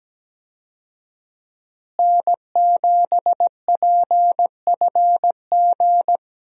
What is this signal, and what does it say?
Morse code, heard as a keyed tone, about 17 wpm, 700 Hz. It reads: N7PFG
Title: USA Call Signs - 5 Characters - Repeated 2x - 17wpm